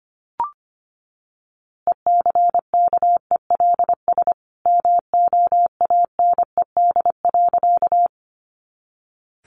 Morse code, heard as a keyed tone, about 25 wpm, 700 Hz. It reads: ECKELS MOANED.
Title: A Sound Of Thunder 25wpm